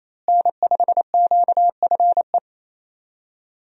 Morse code, heard as a keyed tone, about 28 wpm, 700 Hz. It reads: N5QFE
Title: USA Call Signs - 5 Characters - Repeated 3x - 28wpm